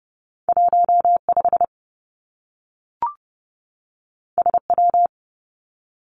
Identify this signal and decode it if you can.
Morse code, heard as a keyed tone, about 32 wpm, 700 Hz. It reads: 1 5 S W
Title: Two Challenging Letter-Number Combinations 30wpm